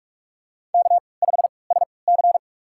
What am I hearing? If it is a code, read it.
Morse code, heard as a keyed tone, about 42 wpm, spaced as for 25 wpm, 700 Hz. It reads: K5SC
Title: USA Call Signs - 3x Character Spacing - 45wpm